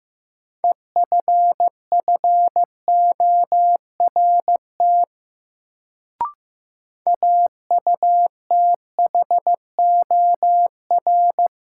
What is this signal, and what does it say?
Morse code, heard as a keyed tone, about 15 wpm, 700 Hz. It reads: EFFORT AUTHOR
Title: Sentences from Top 1200 Words - New Word Drill - 15wpm